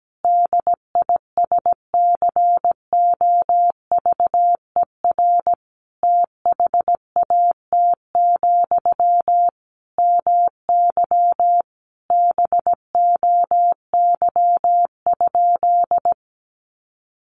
Morse code, heard as a keyed tone, about 17 wpm, 700 Hz. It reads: DISCOVER THAT, MY BOY?
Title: Sentences from Top 1100 Words 17wpm